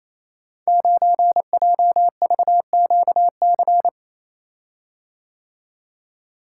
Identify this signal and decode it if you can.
Morse code, heard as a keyed tone, about 28 wpm, 700 Hz. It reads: 9JVQC
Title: Code Groups - 5 Letter:Numbers - 28wpm